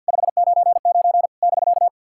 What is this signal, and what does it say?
Morse code, heard as a keyed tone, about 46 wpm, 690 Hz. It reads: 599 <BK>
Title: SOTA QSO Practice 50wpm